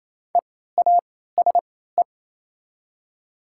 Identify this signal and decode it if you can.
Morse code, heard as a keyed tone, about 27 wpm, spaced as for 16 wpm, 700 Hz. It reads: EASE
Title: 3x Character Spacing - Top 500 Words - 28wpm